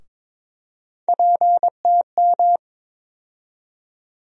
Morse code, heard as a keyed tone, about 22 wpm, 700 Hz. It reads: PTM
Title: USA Call Sign Postfixes - 3 Letters 22wpm